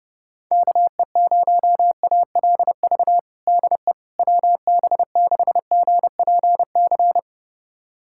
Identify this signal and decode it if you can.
Morse code, heard as a keyed tone, about 32 wpm, 700 Hz. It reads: KE0ALV DE WB6GPC
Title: USA Callsign DE Callsign - 6 Characters - 30wpm